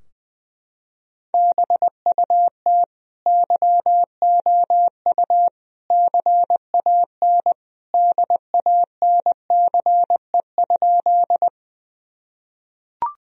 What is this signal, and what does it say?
Morse code, heard as a keyed tone, about 20 wpm, 700 Hz. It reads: BUT YOU CAN DANCE?
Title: Sentences from Top 1000 Words - Review Part 4 - 20wpm